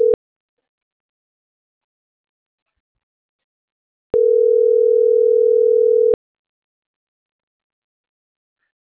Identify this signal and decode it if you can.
Ringback tone (North America), 440 + 480 Hz.